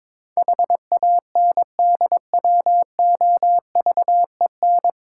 Morse code, heard as a keyed tone, about 22 wpm, 690 Hz. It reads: HANDWOVEN
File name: Difficult Words with Hints - 22wpm